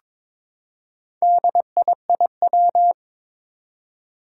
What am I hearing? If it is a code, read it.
Morse code, heard as a keyed tone, about 23 wpm, 700 Hz. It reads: DIIW
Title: Four Challenging Letter-Number Combinations 22wpm